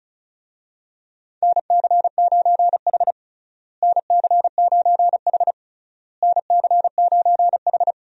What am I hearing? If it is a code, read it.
Morse code, heard as a keyed tone, about 36 wpm, 700 Hz. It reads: NC9H NC9H NC9H
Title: USA Call Signs - 4 Characters - Repeated 3x - 35wpm